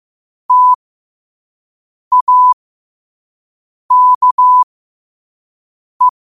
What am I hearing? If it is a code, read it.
Morse code, heard as a keyed tone, about 15 wpm, spaced as for 5 wpm, 1000 Hz. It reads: TAKE
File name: qotdfiles / 5wpm